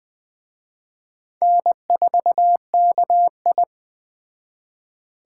Morse code, heard as a keyed tone, about 20 wpm, 700 Hz. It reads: N4KI
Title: Speed Racing - USA 1x2 Call Signs 17wpm